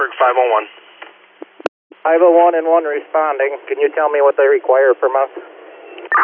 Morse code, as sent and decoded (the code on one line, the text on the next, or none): none